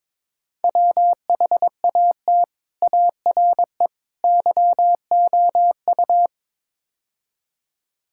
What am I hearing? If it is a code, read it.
Morse code, heard as a keyed tone, about 23 wpm, 700 Hz. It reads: WHAT ARE YOU
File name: practice / 22wpm / Build A Phrase - 5-Word Phrases - 22wpm